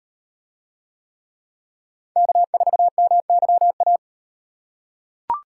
Morse code, heard as a keyed tone, about 38 wpm, 700 Hz. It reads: K4MYA